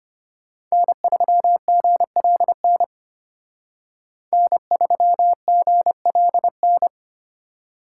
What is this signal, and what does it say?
Morse code, heard as a keyed tone, about 32 wpm, 700 Hz. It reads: N3GLN N3GLN